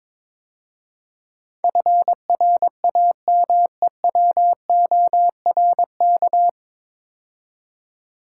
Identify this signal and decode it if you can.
Morse code, heard as a keyed tone, about 22 wpm, 700 Hz. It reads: FRAMEWORK
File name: 9 Letter Words - First 3 Letters Given - 22wpm